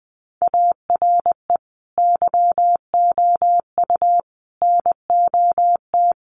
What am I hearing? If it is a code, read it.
Morse code, heard as a keyed tone, about 20 wpm, 700 Hz. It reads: ARE YOU NOT
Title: Sentences from Top 1200 Words 20wpm